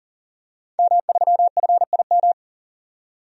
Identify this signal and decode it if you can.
Morse code, heard as a keyed tone, about 40 wpm, 700 Hz. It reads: M3FIM
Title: DXCC-100 Call Signs - 5 Characters - 40wpm